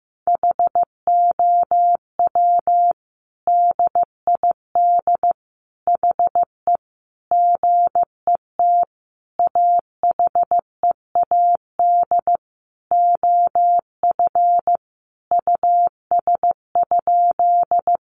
Morse code, heard as a keyed tone, about 15 wpm, 700 Hz. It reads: HOW DID HE GET AHEAD OF US?
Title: Sentences from Top 1100 Words 15wpm